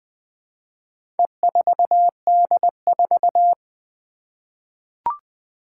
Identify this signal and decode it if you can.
Morse code, heard as a keyed tone, about 20 wpm, 700 Hz. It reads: E4D4
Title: Four Letter-Number Combinations - 20wpm